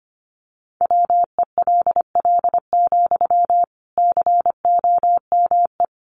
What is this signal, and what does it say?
Morse code, heard as a keyed tone, about 25 wpm, 700 Hz. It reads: WELL, COME
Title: Sentences from Top 1300 Words 25wpm